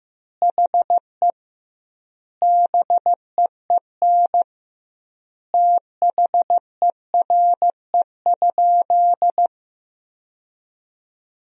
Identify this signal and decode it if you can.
Morse code, heard as a keyed tone, about 15 wpm, 700 Hz. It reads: HE BEEN THERE?
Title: 2x Word-Spacing - Sentences from Top 100 Words - 15wpm